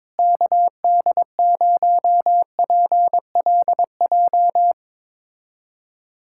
Morse code, heard as a keyed tone, about 22 wpm, 700 Hz. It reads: KD0PLJ
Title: USA Callsign DE Callsign - 6 Characters - 22wpm